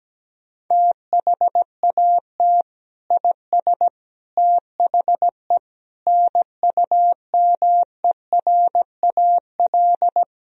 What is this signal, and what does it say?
Morse code, heard as a keyed tone, about 17 wpm, 700 Hz. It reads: THAT IS THE NUMERAL
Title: Sentences from Top 1000 Words - Review Part 2 - 17wpm